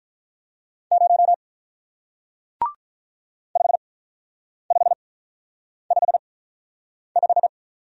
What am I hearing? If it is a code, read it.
Morse code, heard as a keyed tone, about 41 wpm, spaced as for 9 wpm, 700 Hz. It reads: 0 5555
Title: Speed-Racing Single Letter-Number 35wpm